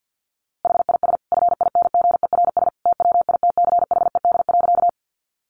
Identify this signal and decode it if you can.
Morse code, heard as a keyed tone, about 50 wpm, 700 Hz. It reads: HIS FINGERS TWITCHED.
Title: A Sound Of Thunder 50wpm